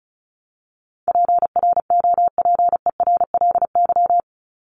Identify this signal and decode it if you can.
Morse code, heard as a keyed tone, about 34 wpm, 700 Hz. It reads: PROPERLY
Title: LY Suffix - Top 100 Words with Base Words 35wpm